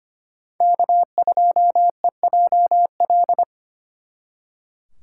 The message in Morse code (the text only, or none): K2EJL